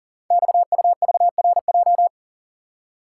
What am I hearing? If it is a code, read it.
Morse code, heard as a keyed tone, about 42 wpm, 700 Hz. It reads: XUVRJ
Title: Code Groups - 5 Letters - 40wpm